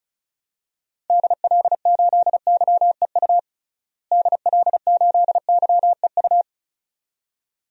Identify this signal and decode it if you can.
Morse code, heard as a keyed tone, about 36 wpm, 700 Hz. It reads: DL8YEU DL8YEU
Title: DXCC-100 Call Signs - 6 Characters - Repeated 2x - 35wpm